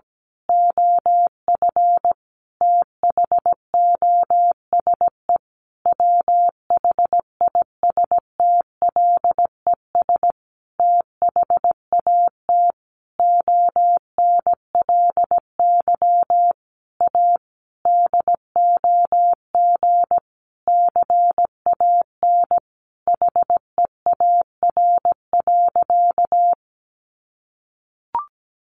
Morse code, heard as a keyed tone, about 17 wpm, 700 Hz. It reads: OF THOSE WHISTLES THAT ONLY A DOG CAN HEAR.
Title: A Sound Of Thunder 17wpm